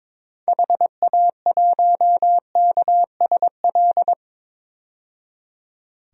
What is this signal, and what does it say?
Morse code, heard as a keyed tone, about 21 wpm, 700 Hz. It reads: HA1KSL
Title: DXCC-100 Call Signs - 6 Characters - 22wpm